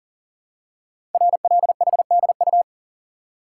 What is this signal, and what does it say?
Morse code, heard as a keyed tone, about 39 wpm, 700 Hz. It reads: RLHDU
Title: Code Groups - 5 Letters - 40wpm